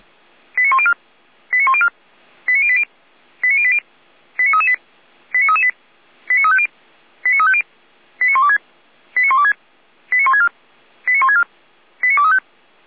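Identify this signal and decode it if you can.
Five-tone selective call (ZVEI), heard as quick runs of five tones, tones of about 70 ms. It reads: 89104 89104 89080 89080 89338 89338 89350 89350 89126 89126 89164 89164 89235
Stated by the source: ZVEI five-tone selective call received by SDR.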